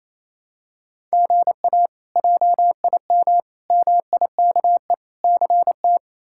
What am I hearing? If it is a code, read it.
Morse code, heard as a keyed tone, about 28 wpm, 700 Hz. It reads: GA JIM MIKE CT
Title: K1USN SST QSO Practice - 28wpm